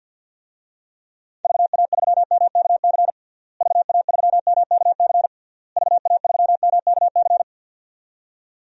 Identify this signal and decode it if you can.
Morse code, heard as a keyed tone, about 49 wpm, 700 Hz. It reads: VA3MKC VA3MKC VA3MKC
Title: DXCC-100 Call Signs - 6 Characters - Repeated 3x - 50wpm